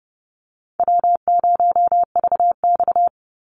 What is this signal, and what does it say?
Morse code, heard as a keyed tone, about 30 wpm, 700 Hz. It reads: W0VX
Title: Easy Call Signs - 30wpm